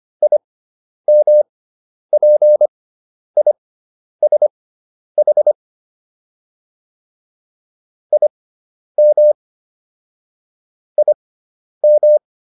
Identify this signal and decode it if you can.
Morse code, heard as a keyed tone, about 25 wpm, spaced as for 10 wpm, 600 Hz. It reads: IMPISH IM IM
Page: intermediate-practice-files